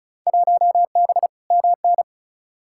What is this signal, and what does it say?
Morse code, heard as a keyed tone, about 36 wpm, 700 Hz. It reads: 1B MN
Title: Field Day Exchanges - 35wpm